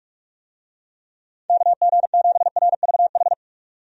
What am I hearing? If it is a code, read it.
Morse code, heard as a keyed tone, about 44 wpm, 700 Hz. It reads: KG7RVH
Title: USA Call Signs - 6 Characters - 45wpm